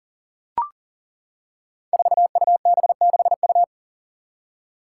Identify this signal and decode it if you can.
Morse code, heard as a keyed tone, about 39 wpm, 700 Hz. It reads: 4UB6U
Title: Five Challenging Letter-Number Combinations 40wpm